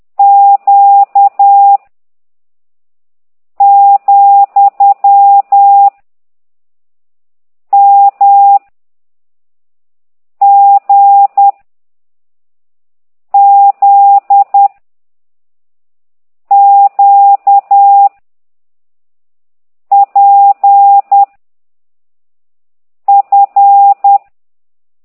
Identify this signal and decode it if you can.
Morse code, heard as a keyed tone, about 10 wpm, 800 Hz. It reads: Q , M G Z Q P F